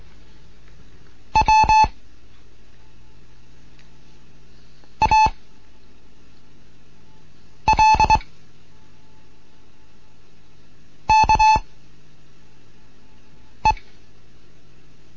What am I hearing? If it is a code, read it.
Morse code, heard as a keyed tone, about 22 wpm, 870 Hz. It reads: W A L K E